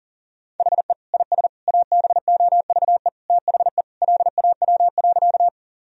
Morse code, heard as a keyed tone, about 37 wpm, 700 Hz. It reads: HE IS ABOVE THE LAW.
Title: Sentences from Top 1000 Words - Review Part 3 - 40wpm